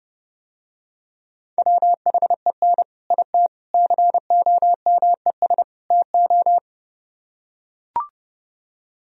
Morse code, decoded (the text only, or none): WHEN IT COMES TO